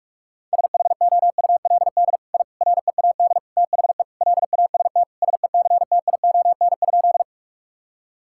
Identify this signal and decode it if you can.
Morse code, heard as a keyed tone, about 43 wpm, 700 Hz. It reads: SHOULD I READ THE LAST SECTION?
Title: Sentences from Top 1000 Words - Review Part 5 - 45wpm